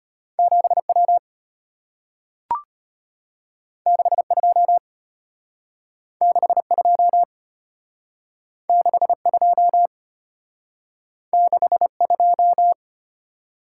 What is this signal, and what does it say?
Morse code, heard as a keyed tone, about 34 wpm, 700 Hz. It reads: 7W 62 62 62 62